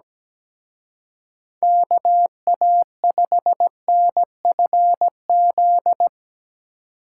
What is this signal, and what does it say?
Morse code, heard as a keyed tone, about 18 wpm, 700 Hz. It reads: KA5NFZ
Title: Speed Racing - USA 2x3 Call Signs 17wpm